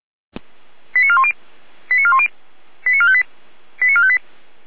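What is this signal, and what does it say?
Five-tone selective call (ZVEI), heard as quick runs of five tones, tones of about 70 ms. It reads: 89410 89410 89458 89458
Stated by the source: ZVEI five-tone selective call received via SDR.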